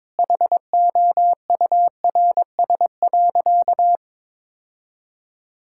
Morse code, heard as a keyed tone, about 22 wpm, 700 Hz. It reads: HOURS.
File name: practice / 22wpm / Easy Sentences with Hints - 22wpm